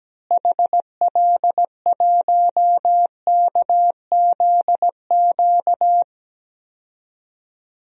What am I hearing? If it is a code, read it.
Morse code, heard as a keyed tone, about 17 wpm, 700 Hz. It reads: HL1KZQ
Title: DXCC-100 Call Signs - 6 Characters - Repeated 2x - 17wpm